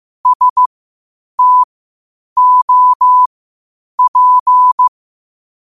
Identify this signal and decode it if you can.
Morse code, heard as a keyed tone, about 15 wpm, spaced as for 8 wpm, 1000 Hz. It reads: STOP
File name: qotdfiles / 8wpm